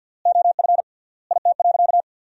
Morse code, heard as a keyed tone, about 53 wpm, 700 Hz. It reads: OF IT.
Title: 3x Word-Spacing - Sentences from Top 500 Words - 50wpm